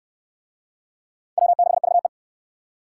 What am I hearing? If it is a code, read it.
Morse code, heard as a keyed tone, about 64 wpm, 700 Hz. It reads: W6VE